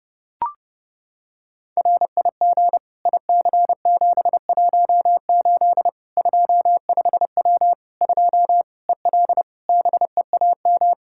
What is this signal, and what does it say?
Morse code, heard as a keyed tone, about 30 wpm, 700 Hz. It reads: RIG IC718 25W 2 EL BEAM
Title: Practice QSO - Exchange 2 - Info - 30wpm